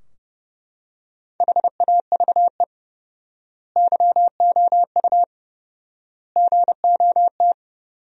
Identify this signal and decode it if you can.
Morse code, heard as a keyed tone, about 30 wpm, 700 Hz. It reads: HAVE YOU GOT
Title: 4x Word-Spacing - Sentences from Top 300 Words - 30wpm